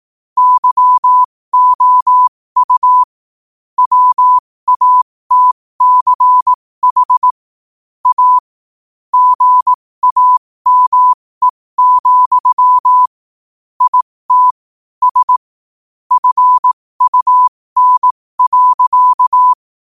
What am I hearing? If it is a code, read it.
Morse code, heard as a keyed tone, about 18 wpm, spaced as for 15 wpm, 1000 Hz. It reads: YOU WATCH A GAME, IT S FUN.
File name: qotdfiles / 15wpm